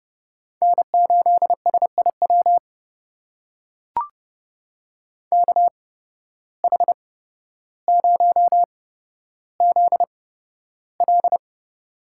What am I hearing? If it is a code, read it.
Morse code, heard as a keyed tone, about 30 wpm, spaced as for 13 wpm, 700 Hz. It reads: N8SIW KH0ZL
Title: USA Call Signs - 5 Characters - 8x Character Spacing - 30wpm